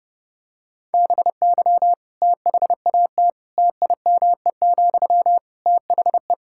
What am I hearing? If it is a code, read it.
Morse code, heard as a keyed tone, about 30 wpm, 700 Hz. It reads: BY THAT TIME, THE
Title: Sentences from Top 1600 Words 30wpm